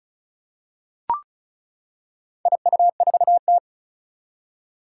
Morse code, heard as a keyed tone, about 36 wpm, 700 Hz. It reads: IU4T
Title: Four Challenging Letter-Number Combinations 35wpm